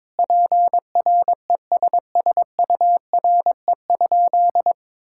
Morse code, heard as a keyed tone, about 22 wpm, 700 Hz. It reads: PRESSURE?
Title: Sentences from Top 1000 Words - Review Part 6 - 22wpm